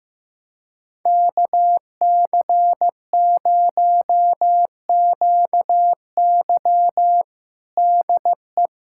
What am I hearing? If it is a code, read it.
Morse code, heard as a keyed tone, about 15 wpm, 700 Hz. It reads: KC0QY DE
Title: USA Callsign DE Callsign - 5 Characters - 15wpm